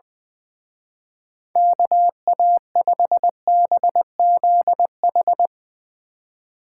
Morse code, heard as a keyed tone, about 20 wpm, 700 Hz. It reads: KA5BZH
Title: Speed Racing - USA 2x3 Call Signs 20wpm